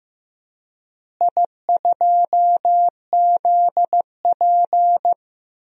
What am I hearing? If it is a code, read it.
Morse code, heard as a keyed tone, about 15 wpm, 700 Hz. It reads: I2ZP
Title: DXCC-100 Call Signs - 4 Characters - Repeated 3x - 15wpm